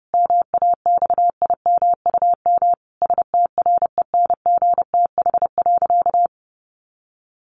Morse code, heard as a keyed tone, about 30 wpm, 700 Hz. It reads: MAXIMUM STRENGTH.
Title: Sentences from Top 1300 Words 30wpm